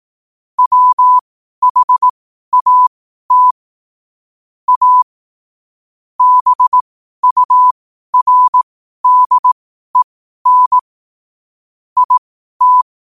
Morse code, heard as a keyed tone, about 18 wpm, spaced as for 12 wpm, 1000 Hz. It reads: WHAT A BURDEN IT
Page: 12 wpm morse code quotes